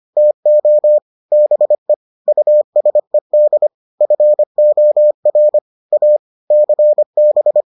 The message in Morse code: TO BE USED FOR A CB

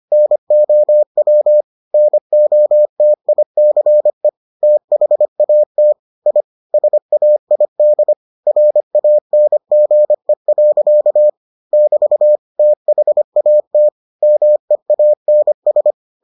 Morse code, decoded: NOW NOTICE THAT I SAID RANGE. <BT> THAT MEANS

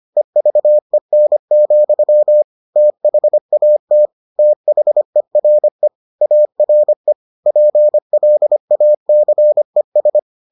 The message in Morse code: EVEN, THAT THERE ARE PLACES

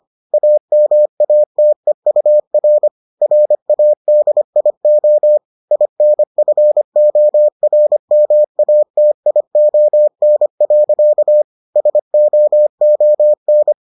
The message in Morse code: AMATEUR RADIO INFORMATION. SOON

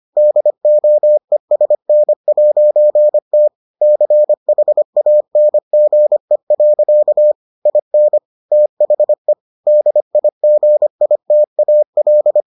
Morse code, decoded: DOESN'T CHANGE. IN THE DIGITAL

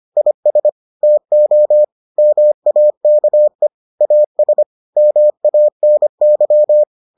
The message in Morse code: IS TO MAKE AS MANY

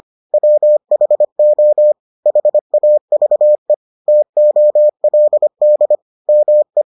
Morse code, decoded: WHO HAVE TOLD ME

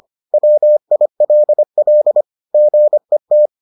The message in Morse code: WILL GET

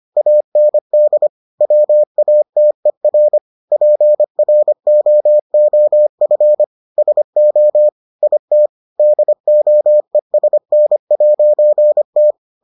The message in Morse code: AND WATER PROOF SO IT DOESN'T

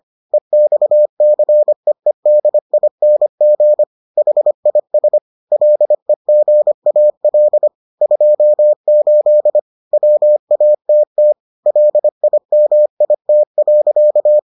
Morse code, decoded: EXCEEDING HIS LEGAL 28 WATT LIMIT.

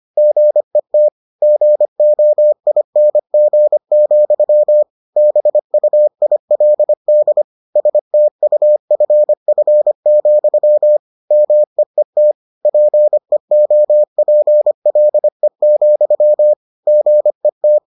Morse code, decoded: GET GOING, BUILD STUFF, MEET PEOPLE, GET

